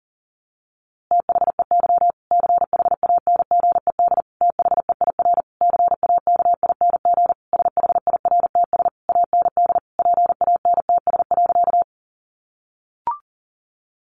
Morse code, heard as a keyed tone, about 40 wpm, 700 Hz. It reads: THEY CHANGED THEIR CAKING SHIRTS AND PANTS.
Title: A Sound Of Thunder 40wpm